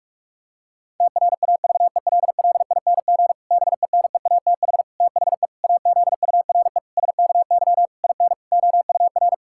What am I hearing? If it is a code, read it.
Morse code, heard as a keyed tone, about 43 wpm, 700 Hz. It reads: TRAVELLING BENEATH THE AZURE SKY IN OUR
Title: Pangram Challenge v2 - 45wpm